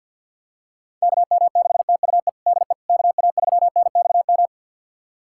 Morse code, heard as a keyed tone, about 51 wpm, 700 Hz. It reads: KM6TUE DE KA3NXM
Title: USA Callsign DE Callsign 50wpm